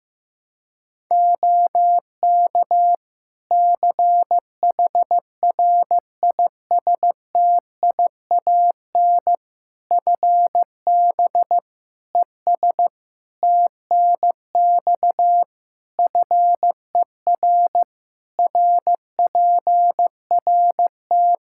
Morse code, heard as a keyed tone, about 15 wpm, 700 Hz. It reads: OK CHRISTIAN FB ES TNX FER RPRT
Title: Practice QSO - Exchange 2 - Info - 15wpm